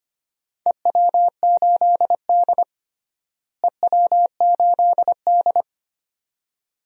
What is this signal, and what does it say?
Morse code, heard as a keyed tone, about 25 wpm, 700 Hz. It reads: EW8D EW8D
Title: DXCC-100 Call Signs - 4 Characters - Repeated 2x - 25wpm